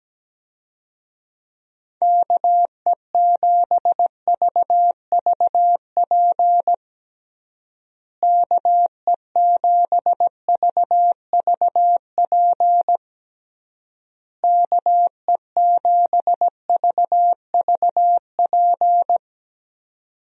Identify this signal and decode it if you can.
Morse code, heard as a keyed tone, about 17 wpm, 700 Hz. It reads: KE7VVP KE7VVP KE7VVP
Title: USA Call Signs - 6 Characters - Repeated 3x - 17wpm